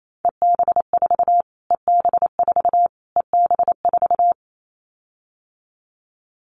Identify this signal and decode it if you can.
Morse code, heard as a keyed tone, about 28 wpm, 700 Hz. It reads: EB4 EB4 EB4
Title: Callsign Prefixes for Top 25 DXCC - Repeated 3x - 28wpm